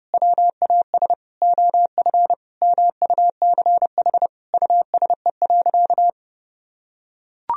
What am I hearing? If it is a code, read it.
Morse code, heard as a keyed tone, about 30 wpm, 700 Hz. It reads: WAS OF MUCH USE.
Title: Sentences from Top 1000 Words - Review Part 3 - 30wpm